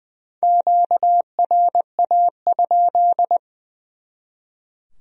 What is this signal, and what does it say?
Morse code, heard as a keyed tone, about 24 wpm, 700 Hz. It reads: QRA?
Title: Speed-Racing Q-Codes - 20wpm